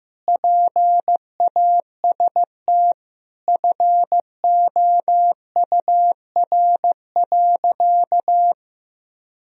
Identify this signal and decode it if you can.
Morse code, heard as a keyed tone, about 15 wpm, 700 Hz. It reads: PAST FOUR.
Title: Sentences from Top 1000 Words - Review Part 3 - 15wpm